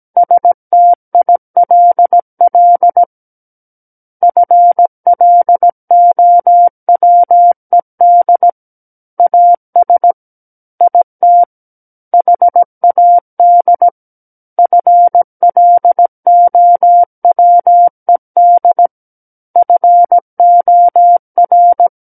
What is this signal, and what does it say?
Morse code, heard as a keyed tone, about 17 wpm, 700 Hz. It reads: STILL FLOWED AS IT HAD FLOWED FOR
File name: Book One / 08-Chapter 8 - 17 WPM